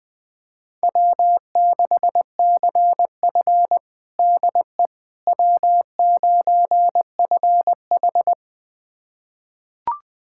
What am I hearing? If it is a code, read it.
Morse code, heard as a keyed tone, about 20 wpm, 700 Hz. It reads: W6CF DE W9FH
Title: USA Callsign DE Callsign - 4 Characters - 20wpm